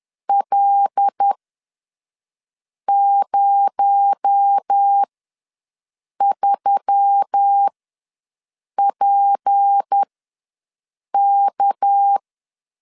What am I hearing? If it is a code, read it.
Morse code, heard as a keyed tone, about 10 wpm, spaced as for 5 wpm, 800 Hz. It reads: L03PK
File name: LEZIONI / 6 PKX0 / 6 PKX0 ws 10wpm 4220